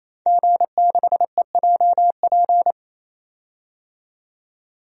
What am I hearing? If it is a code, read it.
Morse code, heard as a keyed tone, about 27 wpm, 700 Hz. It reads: G6EJP
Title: DXCC-100 Call Signs - 5 Characters - 28wpm